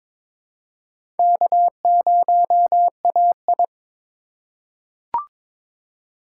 Morse code, heard as a keyed tone, about 22 wpm, 700 Hz. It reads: K0AI
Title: USA Call Signs - 3x Character Spacing - 22wpm